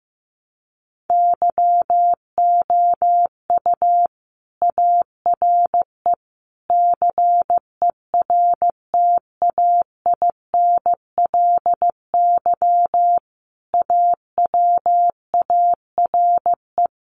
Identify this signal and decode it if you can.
Morse code, heard as a keyed tone, about 15 wpm, 700 Hz. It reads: YOU ARE CERTAINLY AWARE
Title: Sentences from Top 1200 Words 15wpm